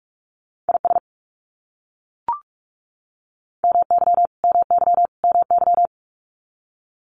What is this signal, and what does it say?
Morse code, heard as a keyed tone, about 44 wpm, 700 Hz. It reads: IS MY MY MY
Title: Top 500 Words Repeated 45wpm